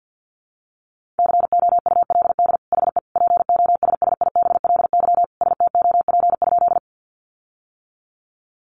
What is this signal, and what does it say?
Morse code, heard as a keyed tone, about 50 wpm, 700 Hz. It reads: COULD HE POSSIBLY STOP?
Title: Sentences from Top 1300 Words 50wpm